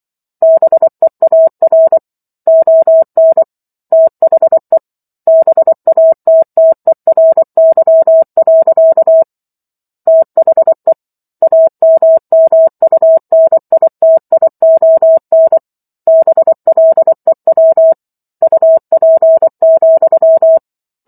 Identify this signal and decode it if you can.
Morse code, heard as a keyed tone, about 24 wpm, 650 Hz. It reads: BEAR ON THE BATTERY. THE AMMUNITION BLEW UP,
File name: Chapter 15 - 24 WPM